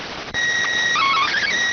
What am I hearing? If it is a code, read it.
SSTV (slow-scan television), PD 290.